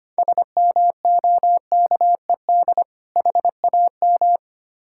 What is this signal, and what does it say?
Morse code, heard as a keyed tone, about 25 wpm, 700 Hz. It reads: SMOKED HAM
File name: practice / 25wpm / Difficult Words with Hints - 25wpm